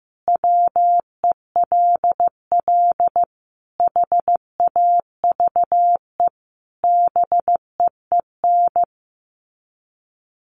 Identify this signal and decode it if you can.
Morse code, heard as a keyed tone, about 15 wpm, 700 Hz. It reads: WELL HAVE BEEN
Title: Sets of 5 Words Encore2 15wpm